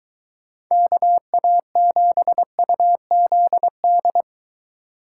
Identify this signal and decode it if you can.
Morse code, heard as a keyed tone, about 23 wpm, 700 Hz. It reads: KA7UZD